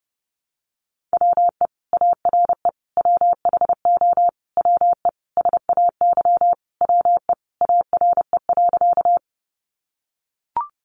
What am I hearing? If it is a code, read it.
Morse code, heard as a keyed tone, about 30 wpm, 700 Hz. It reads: WE ARE WHO WE SAY WE ARE.
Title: Sentences from Top 200 Words 30wpm